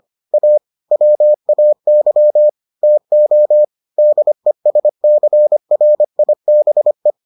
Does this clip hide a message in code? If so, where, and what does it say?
Yes: Morse code from 0.3 to 7.1 s, heard as a keyed tone, about 25 wpm, 600 Hz. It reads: A WAY TO DESCRIBE